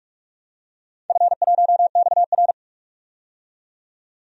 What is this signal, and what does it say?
Morse code, heard as a keyed tone, about 46 wpm, 700 Hz. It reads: F1XR
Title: DXCC-100 Call Signs - 4 Characters - 45wpm